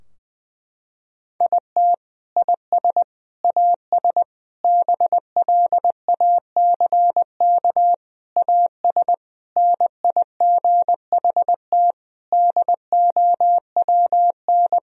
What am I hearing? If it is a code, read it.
Morse code, heard as a keyed tone, about 20 wpm, 700 Hz. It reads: IT IS AS BLACK AS NIGHT DOWN
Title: Sentences from Top 1000 Words - Review Part 2 - 20wpm